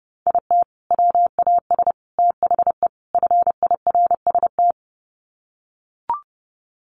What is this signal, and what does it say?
Morse code, heard as a keyed tone, about 30 wpm, 700 Hz. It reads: IT WAS THE FIRST
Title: Sets of 4 Words 30wpm